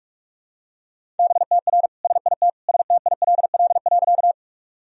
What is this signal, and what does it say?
Morse code, heard as a keyed tone, about 46 wpm, 700 Hz. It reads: BTR SIT STILL.
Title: Sentences from Top 500 Words and Abbreviations 45wpm